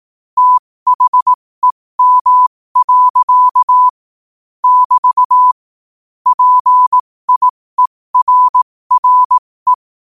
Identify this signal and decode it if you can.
Morse code, heard as a keyed tone, about 18 wpm, spaced as for 15 wpm, 1000 Hz. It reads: THEM. <BT> PIERRE
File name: qotdfiles / 15wpm